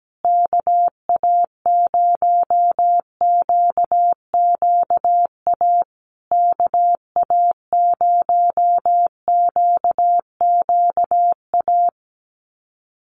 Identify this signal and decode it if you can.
Morse code, heard as a keyed tone, about 17 wpm, 700 Hz. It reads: KA0QQA KA0QQA
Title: USA Call Signs Repeated 3x 17wpm